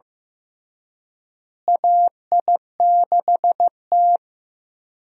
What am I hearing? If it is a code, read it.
Morse code, heard as a keyed tone, about 15 wpm, 700 Hz. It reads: AI6T
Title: USA Call Signs - 4 Characters - Repeated 3x - 15wpm